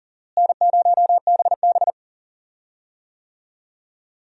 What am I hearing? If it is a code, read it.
Morse code, heard as a keyed tone, about 40 wpm, 700 Hz. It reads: N0BB
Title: USA Call Signs - 4 Characters - Repeated 3x - 40wpm